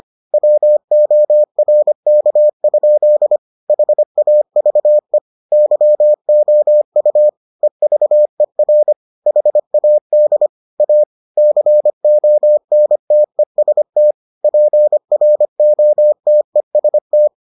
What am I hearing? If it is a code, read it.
Morse code, heard as a keyed tone, about 25 wpm, 600 Hz. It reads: WORK? HAVE YOU EVER HAD A CONTEST PROTEST